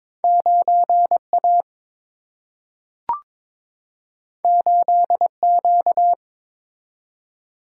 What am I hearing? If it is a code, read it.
Morse code, heard as a keyed tone, about 22 wpm, 700 Hz. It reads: 9 A 8 Q
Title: Single-Letter Single-Number 22wpm